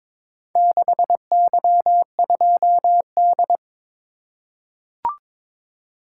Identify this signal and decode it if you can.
Morse code, heard as a keyed tone, about 22 wpm, 700 Hz. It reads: 6Y2D
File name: practice / 22wpm / Four Letter-Number Combinations - 22wpm